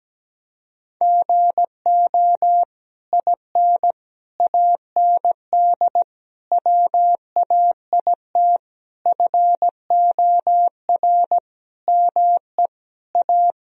Morse code, heard as a keyed tone, about 17 wpm, 700 Hz. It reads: GO IN AND WAIT FOR ME A
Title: Sentences from Top 1000 Words - Review Part 2 - 17wpm